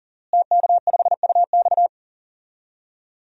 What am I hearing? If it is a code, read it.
Morse code, heard as a keyed tone, about 37 wpm, 700 Hz. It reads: TK5UX
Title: DXCC-100 Call Signs - 5 Characters - Repeated 3x - 40wpm